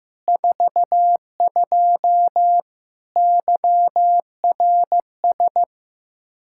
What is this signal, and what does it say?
Morse code, heard as a keyed tone, about 16 wpm, 700 Hz. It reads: 42 YRS
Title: Practice QSO - Exchange 3 - More Info - 15wpm